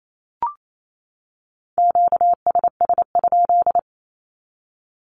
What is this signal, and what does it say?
Morse code, heard as a keyed tone, about 28 wpm, 700 Hz. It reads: QSS?
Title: Q-Codes 28wpm